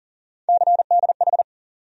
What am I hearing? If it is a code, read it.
Morse code, heard as a keyed tone, about 40 wpm, 700 Hz. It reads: CDH
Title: USA Call Sign Postfixes - 3 Letters 40wpm